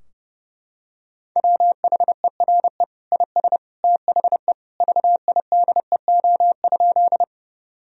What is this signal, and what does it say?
Morse code, heard as a keyed tone, about 30 wpm, 700 Hz. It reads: WHERE IS THE VIDEO?
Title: Sentences from Top 1000 Words - Review Part 4 - 30wpm